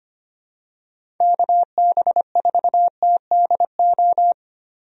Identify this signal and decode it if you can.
Morse code, heard as a keyed tone, about 25 wpm, 700 Hz. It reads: KB4TDO
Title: USA Call Signs - 6 Characters - Repeated 3x - 25wpm